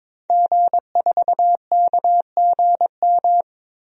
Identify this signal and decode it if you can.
Morse code, heard as a keyed tone, about 22 wpm, 700 Hz. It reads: G4KGM
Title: DXCC-100 Call Signs - 5 Characters - Repeated 2x - 22wpm